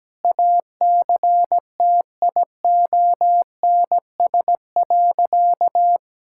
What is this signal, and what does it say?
Morse code, heard as a keyed tone, about 17 wpm, 700 Hz. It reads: ACTIONS.
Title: Sentences from Top 1000 Words - Review Part 6 - 17wpm